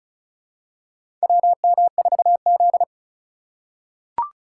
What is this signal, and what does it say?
Morse code, heard as a keyed tone, about 36 wpm, 700 Hz. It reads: WM4Z
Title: USA Call Signs - 4 Characters - 4x Character Spacing - 35wpm